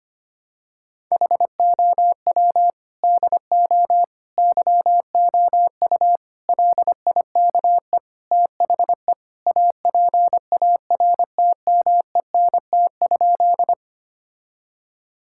Morse code, heard as a keyed tone, about 25 wpm, 700 Hz. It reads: HOW DO YOU LIKE THE APARTMENT?
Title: Sentences from Top 1000 Words - Review Part 6 - 25wpm